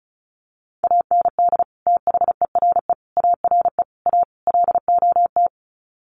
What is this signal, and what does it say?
Morse code, heard as a keyed tone, about 34 wpm, 700 Hz. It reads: AND THERE ARE A LOT
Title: Sets of 5 Words Encore2 35wpm